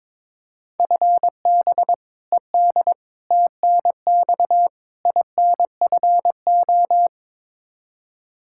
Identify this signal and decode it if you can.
Morse code, heard as a keyed tone, about 22 wpm, 700 Hz. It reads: FB ED TNX INFO
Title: Practice QSO - Exchange 3 - More Info - 22wpm